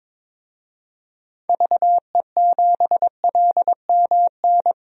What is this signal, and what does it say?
Morse code, heard as a keyed tone, about 22 wpm, 700 Hz. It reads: VE7LMN
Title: DXCC-100 Call Signs - 6 Characters - 22wpm